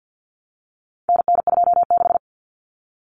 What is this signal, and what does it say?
Morse code, heard as a keyed tone, about 50 wpm, 700 Hz. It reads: NN26
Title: Common Cut Numbers - 4-Digit 50wpm